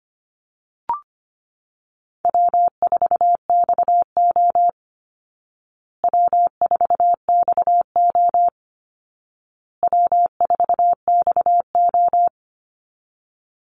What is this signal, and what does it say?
Morse code, heard as a keyed tone, about 25 wpm, 700 Hz. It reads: W4XO W4XO W4XO
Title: Easy Call Signs Repeated 3x - 25wpm